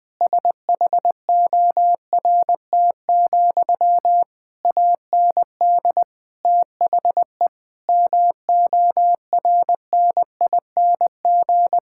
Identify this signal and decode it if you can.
Morse code, heard as a keyed tone, about 20 wpm, 700 Hz. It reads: SHORT, AND THE MORNING
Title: Sentences from Top 1600 Words 20wpm